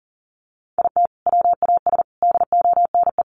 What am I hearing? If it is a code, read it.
Morse code, heard as a keyed tone, about 40 wpm, 700 Hz. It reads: IT WAS DONE
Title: Sentences from Top 1400 Words 40wpm